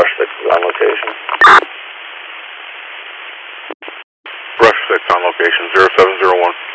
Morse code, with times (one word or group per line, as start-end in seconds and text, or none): none